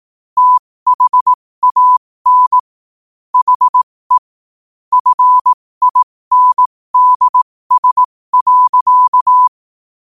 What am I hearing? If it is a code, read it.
Morse code, heard as a keyed tone, about 18 wpm, spaced as for 15 wpm, 1000 Hz. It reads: THAN HE FINDS.
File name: qotdfiles / 15wpm